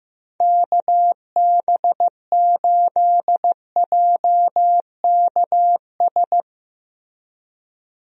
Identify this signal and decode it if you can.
Morse code, heard as a keyed tone, about 15 wpm, 700 Hz. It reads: KB8JKS
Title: USA Call Signs - 6 Characters - Repeated 2x - 15wpm